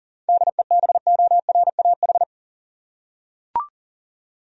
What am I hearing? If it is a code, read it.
Morse code, heard as a keyed tone, about 40 wpm, 700 Hz. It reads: DEBORAH
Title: USA Common Names - 40wpm